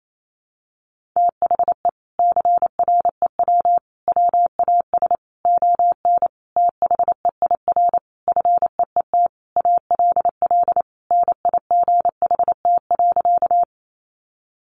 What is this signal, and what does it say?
Morse code, heard as a keyed tone, about 28 wpm, 700 Hz. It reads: THE CREW WAS ON THEIR FEET ALL NIGHT.
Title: Sentences from Top 1500 Words 28wpm